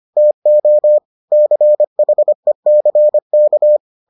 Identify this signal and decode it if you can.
Morse code, heard as a keyed tone, about 25 wpm, 600 Hz. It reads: TO CHECK